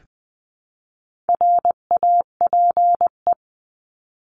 Morse code, heard as a keyed tone, about 20 wpm, 700 Hz. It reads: RAPE
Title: 259 Lesson 42d - Words 18-20wpm